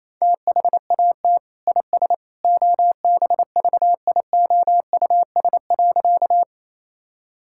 Morse code, heard as a keyed tone, about 28 wpm, 700 Hz. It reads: THAT IS OBVIOUS.
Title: Sentences from Top 1600 Words 28wpm